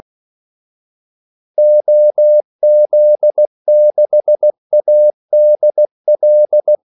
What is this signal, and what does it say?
Morse code, heard as a keyed tone, about 16 wpm, 600 Hz. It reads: OZ6ADL
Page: Kaldesignaler 16 wpm